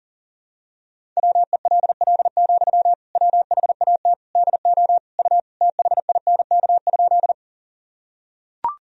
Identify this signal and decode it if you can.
Morse code, heard as a keyed tone, about 40 wpm, 700 Hz. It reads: WELL, WHAT DO U THINK?
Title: Sentences from Top 200 Words and Abbreviations 40wpm